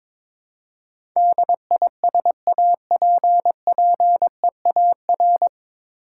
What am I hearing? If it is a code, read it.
Morse code, heard as a keyed tone, about 22 wpm, 700 Hz. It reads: DISAPPEAR
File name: 9 Letter Words - First 3 Letters Given - 22wpm